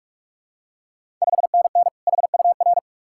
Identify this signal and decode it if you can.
Morse code, heard as a keyed tone, about 46 wpm, 700 Hz. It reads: 5NN HUR
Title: Ontario QSO Party Exchange 45wpm